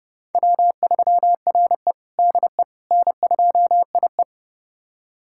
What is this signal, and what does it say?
Morse code, heard as a keyed tone, about 32 wpm, 700 Hz. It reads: W3RE DE N2IE
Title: USA Callsign DE Callsign - 4 Characters - 30wpm